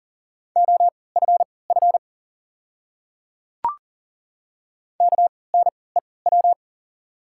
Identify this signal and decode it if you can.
Morse code, heard as a keyed tone, about 38 wpm, spaced as for 22 wpm, 700 Hz. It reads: OFF KNEW
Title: 3x Character Spacing - Top 500 Words - 40wpm